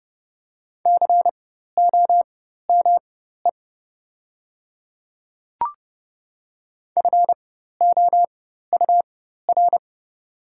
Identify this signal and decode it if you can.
Morse code, heard as a keyed tone, about 31 wpm, spaced as for 14 wpm, 700 Hz. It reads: COME FOUR
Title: 4x Character Spacing - Top 300 Words - 30wpm